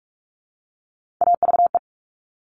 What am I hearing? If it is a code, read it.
Morse code, heard as a keyed tone, about 46 wpm, 700 Hz. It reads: AVE
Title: Top Three Letter Combinations in Words 45wpm